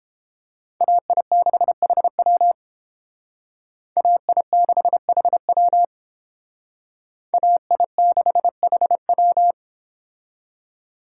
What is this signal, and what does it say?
Morse code, heard as a keyed tone, about 30 wpm, 700 Hz. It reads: AI6HW AI6HW AI6HW